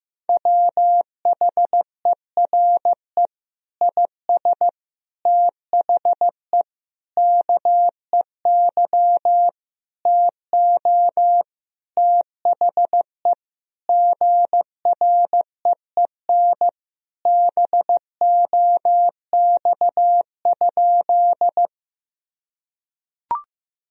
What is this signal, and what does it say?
Morse code, heard as a keyed tone, about 15 wpm, 700 Hz. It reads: WHERE IS THE KEY TO THE GREEN BOX?
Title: Sentences from Top 1000 Words - Review Part 4 - 15wpm